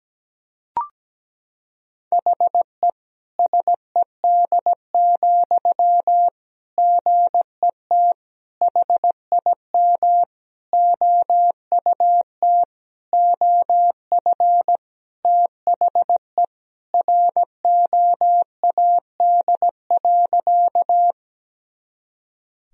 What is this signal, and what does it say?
Morse code, heard as a keyed tone, about 17 wpm, 700 Hz. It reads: HE SED, GET HIM OUT OF THE ROAD.
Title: Sentences from Top 400 Words and Abbreviations 17wpm